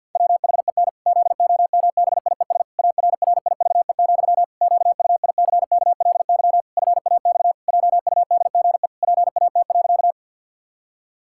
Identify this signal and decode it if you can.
Morse code, heard as a keyed tone, about 52 wpm, 700 Hz. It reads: WHEN ZOMBIES ARRIVE, QUICKLY FAX JUDGE PAT.
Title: Pangram Challenge v2 - 50wpm